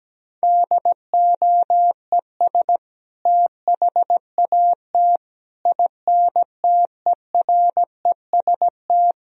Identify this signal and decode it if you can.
Morse code, heard as a keyed tone, about 17 wpm, 700 Hz. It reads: DOES THAT INTEREST
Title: Sentences from Top 1000 Words - Review Part 2 - 17wpm